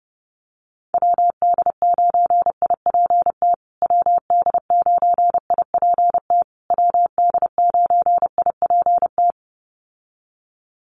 Morse code, heard as a keyed tone, about 28 wpm, 700 Hz. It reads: WD9IPT WD9IPT WD9IPT
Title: USA Call Signs Repeated 3x 30wpm